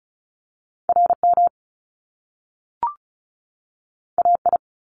Morse code, heard as a keyed tone, about 36 wpm, 700 Hz. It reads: RM AI
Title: Top Two Letter Combinations in Words 35wpm